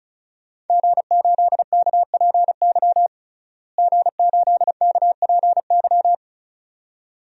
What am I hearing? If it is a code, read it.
Morse code, heard as a keyed tone, about 36 wpm, 700 Hz. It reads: G8KPY G8KPY
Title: DXCC-100 Call Signs - 5 Characters - Repeated 2x - 35wpm